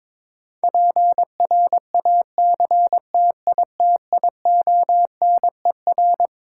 Morse code, heard as a keyed tone, about 22 wpm, 700 Hz. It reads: PRACTITIONER
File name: Twelve Letter Words - 22wpm